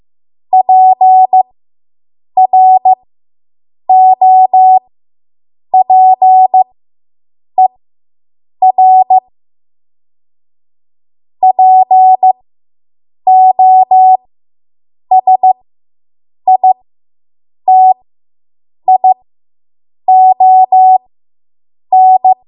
Morse code, heard as a keyed tone, about 15 wpm, spaced as for 7 wpm, 750 Hz. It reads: PROPER POSITION